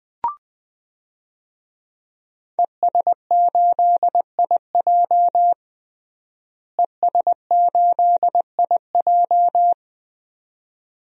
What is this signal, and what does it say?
Morse code, heard as a keyed tone, about 20 wpm, 700 Hz. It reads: ES8IJ ES8IJ
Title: DXCC-100 Call Signs - 5 Characters - Repeated 2x - 20wpm